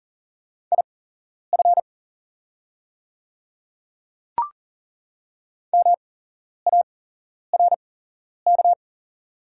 Morse code, heard as a keyed tone, about 37 wpm, spaced as for 11 wpm, 700 Hz. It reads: IF MARK
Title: 8x Character Spacing - Top 300 Words - 40wpm